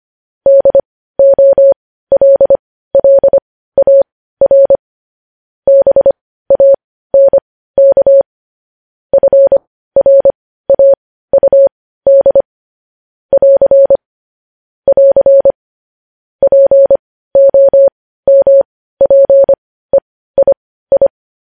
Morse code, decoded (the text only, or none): DOLLAR BANK FRAUD <AR> <AR> POMPEII